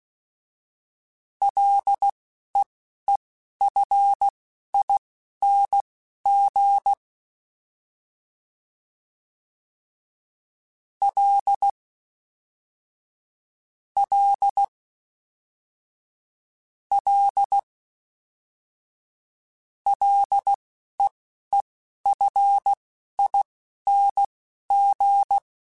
Morse code, heard as a keyed tone, about 16 wpm, spaced as for 11 wpm, 800 Hz. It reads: LEEFING L L L LEEFING
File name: F and L / F and L mp3 / F and L Prefixes 3x2 Twice @10